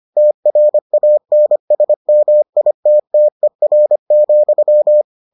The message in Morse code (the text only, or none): TRANSMITTER,